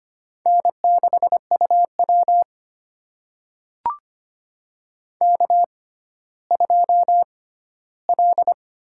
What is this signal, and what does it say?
Morse code, heard as a keyed tone, about 25 wpm, spaced as for 14 wpm, 700 Hz. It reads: N6UW K2L
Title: USA Call Signs - 4 Characters - 6x Character Spacing - 25wpm